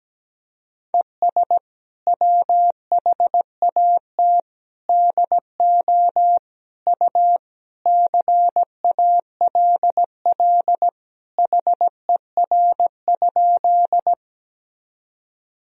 Morse code, heard as a keyed tone, about 17 wpm, 700 Hz. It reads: ES WHAT DO U CALL HER?
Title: Sentences from Top 100 Words and Abbreviations 17wpm